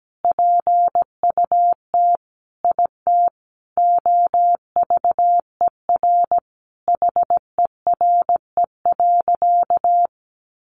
Morse code, heard as a keyed tone, about 17 wpm, 700 Hz. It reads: PUT IT OVER HERE.
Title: Sentences from Top 1500 Words 17wpm